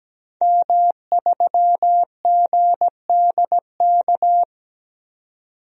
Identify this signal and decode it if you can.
Morse code, heard as a keyed tone, about 17 wpm, 700 Hz. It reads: M3GDK
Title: DXCC-100 Call Signs - 5 Characters - 17wpm